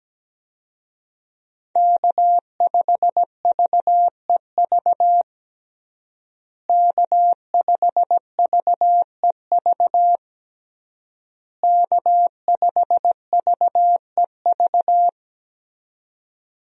Morse code, heard as a keyed tone, about 17 wpm, 700 Hz. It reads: K5VEV K5VEV K5VEV
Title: USA Call Signs - 5 Characters - Repeated 3x - 17wpm